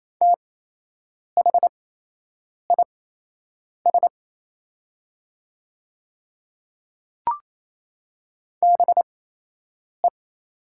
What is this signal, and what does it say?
Morse code, heard as a keyed tone, about 28 wpm, spaced as for 8 wpm, 700 Hz. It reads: THIS BE
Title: 8x Character Spacing - Top 500 Words - 28wpm